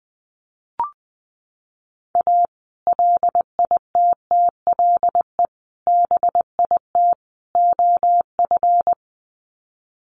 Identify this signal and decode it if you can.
Morse code, heard as a keyed tone, about 20 wpm, 700 Hz. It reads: A LITTLE BIT OF
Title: Sets of 4 Words 20wpm